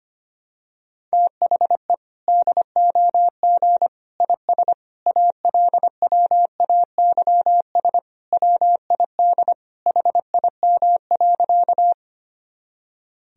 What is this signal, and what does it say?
Morse code, heard as a keyed tone, about 25 wpm, 700 Hz. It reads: THE DOG IS ALWAYS WID HIM.
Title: Sentences from Top 400 Words and Abbreviations 25wpm